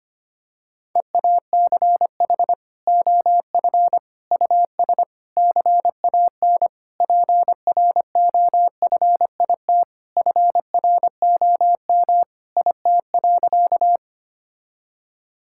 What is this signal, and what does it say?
Morse code, heard as a keyed tone, about 25 wpm, 700 Hz. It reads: EACH OF US CAN PROFIT FROM IT.
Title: Sentences from Top 1000 Words - Review Part 5 - 25wpm